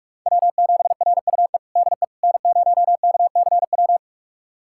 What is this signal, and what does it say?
Morse code, heard as a keyed tone, about 47 wpm, 700 Hz. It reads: W7RUE DE N0KCW
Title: USA Callsign DE Callsign - 5 Characters - 45wpm